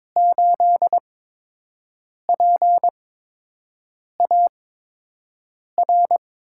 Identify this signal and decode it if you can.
Morse code, heard as a keyed tone, about 21 wpm, spaced as for 6 wpm, 700 Hz. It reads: 8PAR
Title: USA Call Signs - 5 Characters - 8x Character Spacing - 22wpm